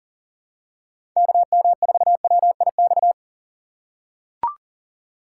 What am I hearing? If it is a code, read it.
Morse code, heard as a keyed tone, about 40 wpm, 700 Hz. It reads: KM4WIX
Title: USA Call Signs - 6 Characters - 6x Character Spacing - 40wpm